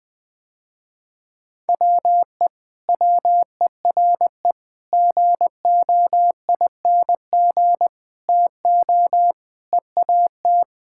Morse code, heard as a keyed tone, about 20 wpm, 700 Hz. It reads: WE WERE GOING TO EAT
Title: Sentences with 4-Word Phrases - Spoken Beginning - 20wpm